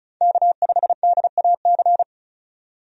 Morse code, heard as a keyed tone, about 36 wpm, 700 Hz. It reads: K5DAC
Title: USA Call Signs - 5 Characters - 35wpm